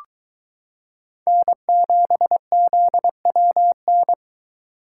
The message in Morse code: N7ZWN

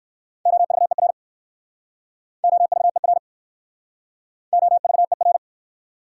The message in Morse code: OVER OVER OVER

